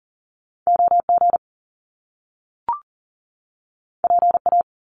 Morse code, heard as a keyed tone, about 40 wpm, 700 Hz. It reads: OG PA
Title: Top Two Letter Combinations in Words 40wpm